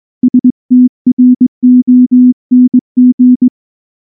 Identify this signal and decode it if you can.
Morse code, heard as a keyed tone, about 19 wpm, 250 Hz. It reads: STRONG
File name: morse / morse-training-20WPM